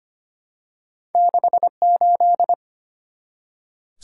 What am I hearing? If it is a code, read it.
Morse code, heard as a keyed tone, about 25 wpm, 700 Hz. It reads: 6 8
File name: Two Letter-Number Combinations - 25wpm